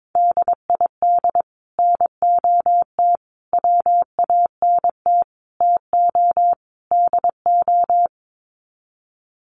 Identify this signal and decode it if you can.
Morse code, heard as a keyed tone, about 22 wpm, 700 Hz. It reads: DID NOT WANT TO DO
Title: Sets of 5 Words Encore 22wpm